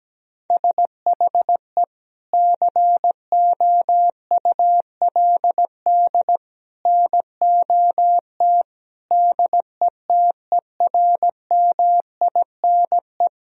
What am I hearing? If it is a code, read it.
Morse code, heard as a keyed tone, about 17 wpm, 700 Hz. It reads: SHE COULD NOT DETERMINE
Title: Sentences from Top 1000 Words - Review Part 4 - 17wpm